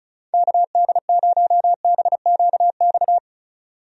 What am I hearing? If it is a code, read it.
Morse code, heard as a keyed tone, about 36 wpm, 700 Hz. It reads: KD0BQX
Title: USA Call Signs - 6 Characters - 35wpm